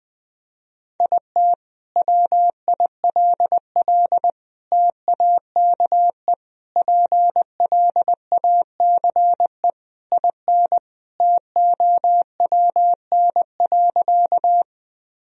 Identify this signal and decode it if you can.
Morse code, heard as a keyed tone, about 20 wpm, 700 Hz. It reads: IT WILL TAKE PLACE IN TOWN.
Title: Sentences from Top 1000 Words - Review Part 2 - 20wpm